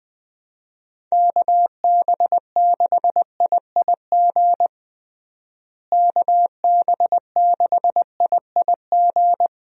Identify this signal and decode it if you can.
Morse code, heard as a keyed tone, about 20 wpm, 700 Hz. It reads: KB6IIG KB6IIG
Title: USA Call Signs - 6 Characters - Repeated 2x - 20wpm